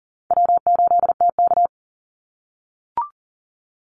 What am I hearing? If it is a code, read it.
Morse code, heard as a keyed tone, about 40 wpm, 700 Hz. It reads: W8TK
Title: Easy Call Signs - 40wpm